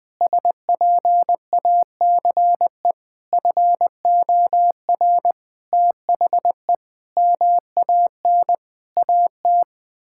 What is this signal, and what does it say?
Morse code, heard as a keyed tone, about 20 wpm, 700 Hz. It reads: SPACE FOR THE MAN AT
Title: Sentences from Top 1000 Words - Review Part 5 - 20wpm